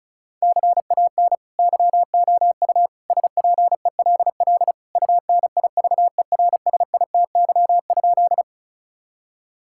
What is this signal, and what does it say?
Morse code, heard as a keyed tone, about 34 wpm, 700 Hz. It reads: CAN YOU SPELL UNIVERSITY?
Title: Sentences from Top 1000 Words - Review Part 5 - 35wpm